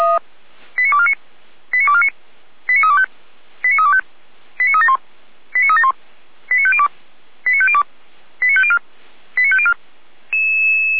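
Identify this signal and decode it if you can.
Five-tone selective call (ZVEI), heard as quick runs of five tones, tones of about 70 ms. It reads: 89249 89249 89325 89325 89371 89371 89502 89502 89504 89504
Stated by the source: ZVEI five-tone call received via SDR, fire brigade Orlamünde (feuerwehr-orlamuende) receiver.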